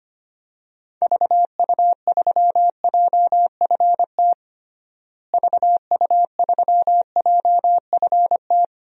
Morse code, heard as a keyed tone, about 25 wpm, 700 Hz. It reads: VU3JFT VU3JFT
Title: DXCC-100 Call Signs - 6 Characters - Repeated 3x - 25wpm